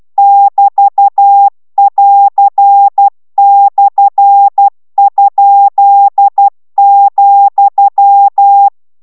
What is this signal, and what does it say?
Morse code, heard as a keyed tone, about 12 wpm, 800 Hz. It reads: <BT><AR>/?,